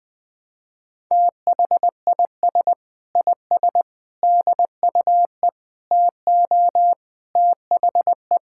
Morse code, heard as a keyed tone, about 21 wpm, 700 Hz. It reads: THIS IS DUE TO THE
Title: Sentences with 8-Word Phrases - 20wpm